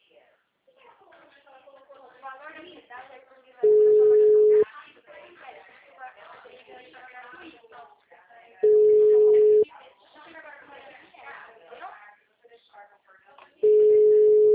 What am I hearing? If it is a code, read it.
Ringback tone (Europe, 425 Hz), with speech.